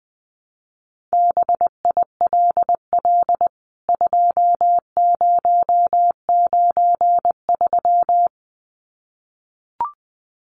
Morse code, heard as a keyed tone, about 20 wpm, 700 Hz. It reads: BILL 2093
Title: CWT Exchange 20wpm